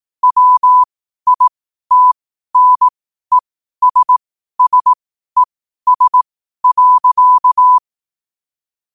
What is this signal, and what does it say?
Morse code, heard as a keyed tone, about 18 wpm, spaced as for 12 wpm, 1000 Hz. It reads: WITNESSES.
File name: qotdfiles / 12wpm